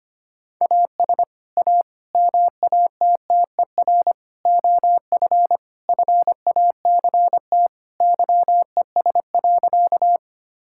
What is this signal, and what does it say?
Morse code, heard as a keyed tone, about 25 wpm, 700 Hz. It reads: AS A MATTER OF FACT YES.
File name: practice / 25wpm / Sentences with 4-Word Phrases - 25wpm